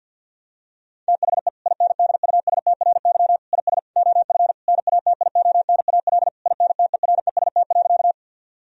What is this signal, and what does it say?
Morse code, heard as a keyed tone, about 50 wpm, 700 Hz. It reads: THE INDUSTRY IS OF NATIONAL INTEREST.
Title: Sentences from Top 1000 Words - Review Part 4 - 50wpm